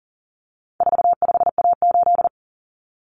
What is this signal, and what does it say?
Morse code, heard as a keyed tone, about 40 wpm, 700 Hz. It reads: 45A8
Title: Common Cut Numbers - 4-Digit 40wpm